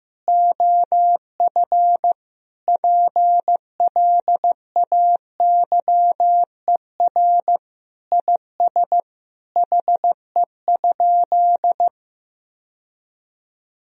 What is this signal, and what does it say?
Morse code, heard as a keyed tone, about 15 wpm, 700 Hz. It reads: OF PLAYER IS HE?
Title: Sentences from Top 1000 Words - Review Part 4 - 15wpm